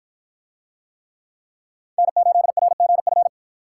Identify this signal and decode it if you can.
Morse code, heard as a keyed tone, about 53 wpm, 700 Hz. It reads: N8RGF